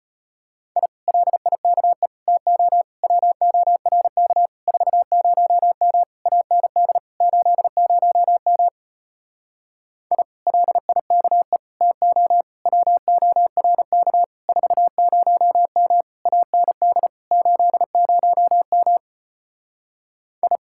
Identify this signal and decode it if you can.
Morse code, heard as a keyed tone, about 33 wpm, 700 Hz. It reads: I LIKE TO WORK 40M AND 80M I LIKE TO WORK 40M AND 80M I